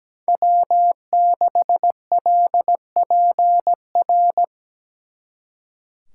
Morse code, heard as a keyed tone, about 20 wpm, 700 Hz. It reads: W6LPR
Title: Speed Racing - USA 1x3 Call Signs 17wpm